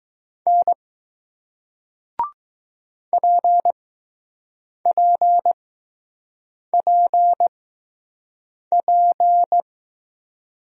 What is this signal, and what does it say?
Morse code, heard as a keyed tone, about 20 wpm, spaced as for 6 wpm, 700 Hz. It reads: N PPPP